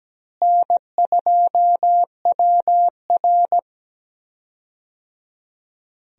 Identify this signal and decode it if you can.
Morse code, heard as a keyed tone, about 17 wpm, 700 Hz. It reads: N2WR
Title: USA Call Signs - 4 Characters - Repeated 3x - 17wpm